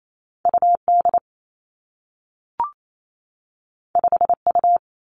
Morse code, heard as a keyed tone, about 28 wpm, 700 Hz. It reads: U D 5 U
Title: Two Challenging Letter-Number Combinations 28wpm